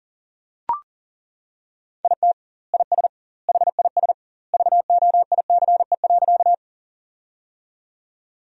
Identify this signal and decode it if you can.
Morse code, heard as a keyed tone, about 40 wpm, 700 Hz. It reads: IT IS HIS VOICE.
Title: 2x Word-Spacing - Sentences from Top 500 Words - 40wpm